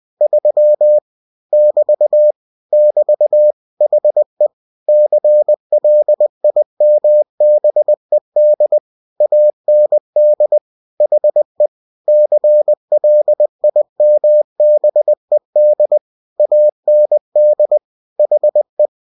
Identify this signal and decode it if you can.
Morse code, heard as a keyed tone, about 20 wpm, 600 Hz. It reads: 3 <BT> <BT> HE CLIMBED AND HE CLIMBED AND HE
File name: cw-books / winnie-the-pooh / winnie-20wpm-page-0003